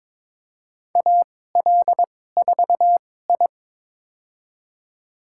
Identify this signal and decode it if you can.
Morse code, heard as a keyed tone, about 21 wpm, spaced as for 16 wpm, 700 Hz. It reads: AL4I
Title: USA Call Signs - 4 Characters - 2x Character Spacing - 22wpm